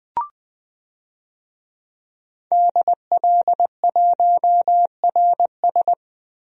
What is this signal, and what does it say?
Morse code, heard as a keyed tone, about 20 wpm, 700 Hz. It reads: DL1RS
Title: DXCC-100 Call Signs - 5 Characters - Repeated 3x - 20wpm